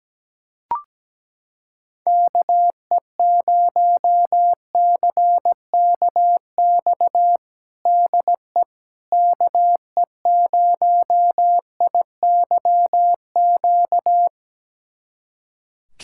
Morse code, heard as a keyed tone, about 17 wpm, 700 Hz. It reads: KE0CKX DE KE0IYQ
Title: USA Callsign DE Callsign - 6 Characters - 17wpm